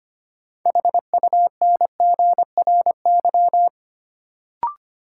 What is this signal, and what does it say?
Morse code, heard as a keyed tone, about 25 wpm, 700 Hz. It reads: HUNGRY
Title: Sentences from Top 1400 Words - New Word Drill - 25wpm